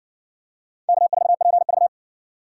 Morse code, heard as a keyed tone, about 57 wpm, 700 Hz. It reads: K4PV